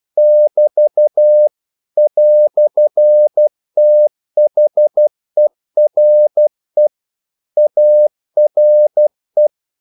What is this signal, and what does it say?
Morse code, heard as a keyed tone, about 12 wpm, 600 Hz. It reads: <BT> "THERE ARE
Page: English quotations in CW (telegraphy - morse)